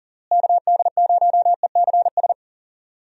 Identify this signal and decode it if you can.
Morse code, heard as a keyed tone, about 40 wpm, 700 Hz. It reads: KD0ECS
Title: USA Call Signs - 6 Characters - Repeated 3x - 40wpm